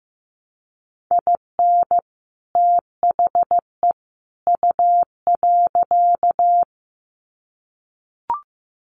Morse code, heard as a keyed tone, about 15 wpm, 700 Hz. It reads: IN THE U.
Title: Sets of 3 Words 15wpm